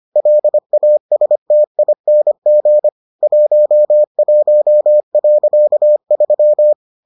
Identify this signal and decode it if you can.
Morse code, heard as a keyed tone, about 25 wpm, 600 Hz. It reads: LASTING 11.3